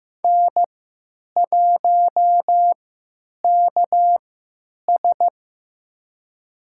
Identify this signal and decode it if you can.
Morse code, heard as a keyed tone, about 15 wpm, spaced as for 9 wpm, 700 Hz. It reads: N1KS
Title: USA Call Signs - 4 Characters - 3x Character Spacing - 15wpm